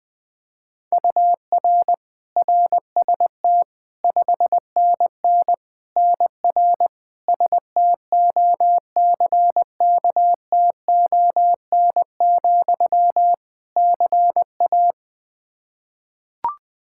Morse code, heard as a keyed tone, about 20 wpm, 700 Hz. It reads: UR RST 5NN NR STOCKTON, CA
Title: Practice QSO - Exchange 1 - Beginning - 20wpm